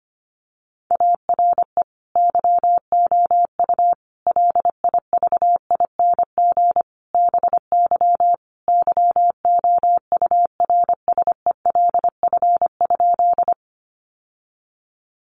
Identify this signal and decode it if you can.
Morse code, heard as a keyed tone, about 25 wpm, 700 Hz. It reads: ARE YOU LIVING BY YOURSELF?
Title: Sentences from Top 1100 Words 25wpm